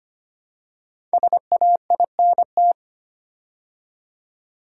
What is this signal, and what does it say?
Morse code, heard as a keyed tone, about 25 wpm, 700 Hz. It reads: SAINT
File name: 5 Letter Words - First 3 Letters Given - 25wpm